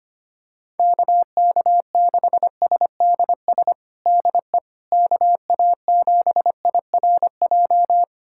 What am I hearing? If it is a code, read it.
Morse code, heard as a keyed tone, about 25 wpm, 700 Hz. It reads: KK6SDS DE KA7IRJ
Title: USA Callsign DE Callsign - 6 Characters - 25wpm